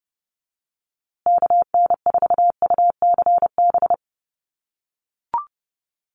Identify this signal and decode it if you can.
Morse code, heard as a keyed tone, about 30 wpm, 700 Hz. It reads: KN4UCB
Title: USA Call Signs Repeated 2x 30wpm